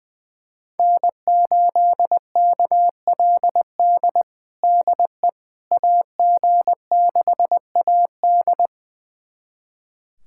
Morse code, heard as a keyed tone, about 21 wpm, 700 Hz. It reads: N8KLD DE AG6AD
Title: USA Callsign DE Callsign - 5 Characters - 20wpm